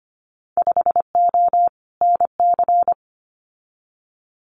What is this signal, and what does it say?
Morse code, heard as a keyed tone, about 25 wpm, 700 Hz. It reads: 5O NC
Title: Winter Field Day Exchanges 25wpm